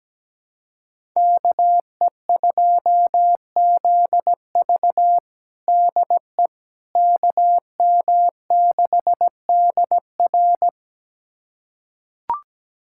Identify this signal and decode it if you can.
Morse code, heard as a keyed tone, about 17 wpm, 700 Hz. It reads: KE2ZV DE KM6DR
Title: USA Callsign DE Callsign - 5 Characters - 17wpm